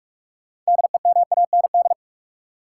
Morse code, heard as a keyed tone, about 46 wpm, 700 Hz. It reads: DEMAND
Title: Sentences from Top 800 Words - New Word Drill - 45wpm